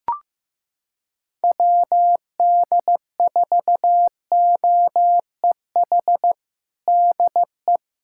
Morse code, heard as a keyed tone, about 15 wpm, 700 Hz. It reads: WD4OEH DE
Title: USA Callsign DE Callsign - 6 Characters - 15wpm